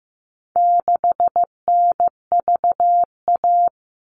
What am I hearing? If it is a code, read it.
Morse code, heard as a keyed tone, about 15 wpm, 700 Hz. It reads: 6NVA
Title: All Cut Numbers - 4-Digit 15wpm